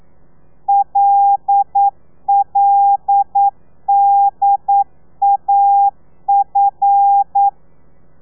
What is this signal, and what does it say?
Morse code, heard as a keyed tone, about 9 wpm, 800 Hz. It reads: LLDAF
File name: telegrafia / i2tfj / Lez5 40